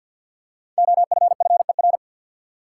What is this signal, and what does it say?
Morse code, heard as a keyed tone, about 48 wpm, 700 Hz. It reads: OFFER
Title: Five Letter Words - 50wpm